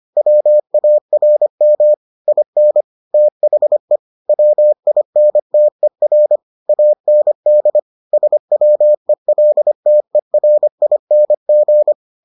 Morse code, heard as a keyed tone, about 25 wpm, 600 Hz. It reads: WARM IN THE WINTER AND SWELTERING